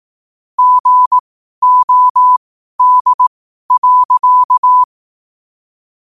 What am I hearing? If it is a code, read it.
Morse code, heard as a keyed tone, about 18 wpm, spaced as for 12 wpm, 1000 Hz. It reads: GOD.
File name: qotdfiles / 12wpm